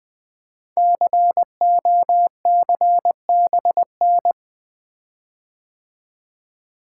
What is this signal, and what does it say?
Morse code, heard as a keyed tone, about 21 wpm, 700 Hz. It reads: COCBN
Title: Code Groups - 5 Letters - 20wpm